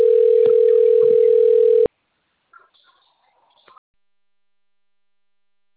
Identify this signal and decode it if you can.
Ringback tone (North America), 440 + 480 Hz.